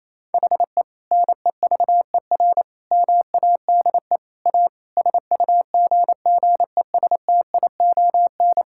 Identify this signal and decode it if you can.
Morse code, heard as a keyed tone, about 28 wpm, 700 Hz. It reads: HE NEVER MADE A SUGGESTION
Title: Sentences from Top 1600 Words 28wpm